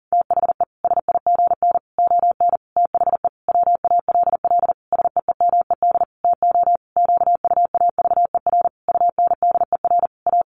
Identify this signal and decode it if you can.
Morse code, heard as a keyed tone, about 40 wpm, 700 Hz. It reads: THE SIGN ON THE WALL SEEMED TO QUAVER UNDER A
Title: A Sound Of Thunder 40wpm